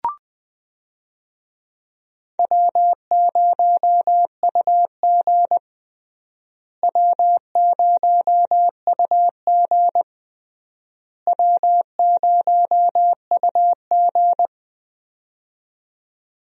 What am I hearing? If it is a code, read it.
Morse code, heard as a keyed tone, about 20 wpm, 700 Hz. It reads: W0UG W0UG W0UG
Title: USA Call Signs - 4 Characters - Repeated 3x - 20wpm